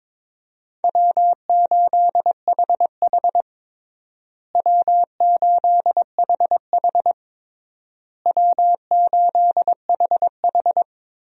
Morse code, heard as a keyed tone, about 22 wpm, 700 Hz. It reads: W8HH W8HH W8HH
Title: USA Call Signs - 4 Characters - Repeated 3x - 22wpm